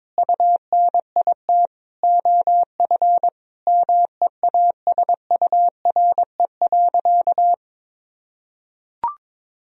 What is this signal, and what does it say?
Morse code, heard as a keyed tone, about 21 wpm, 700 Hz. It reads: UNIT OF MEASURE.
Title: Sentences from Top 1000 Words - Review Part 2 - 22wpm